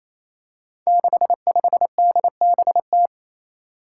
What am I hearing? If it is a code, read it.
Morse code, heard as a keyed tone, about 28 wpm, 700 Hz. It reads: 65DBT
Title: Five Challenging Letter-Number Combinations 28wpm